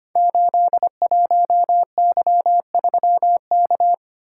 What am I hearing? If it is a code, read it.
Morse code, heard as a keyed tone, about 25 wpm, 700 Hz. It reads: 81Y3K
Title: Code Groups - 5 Letter:Numbers - 25wpm